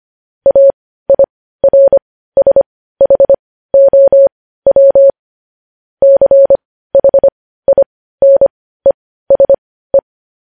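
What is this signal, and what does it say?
Morse code, heard as a keyed tone, about 25 wpm, spaced as for 15 wpm, 550 Hz. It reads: AIRSHOW CHINESE
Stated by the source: Morse code news headlines at 15.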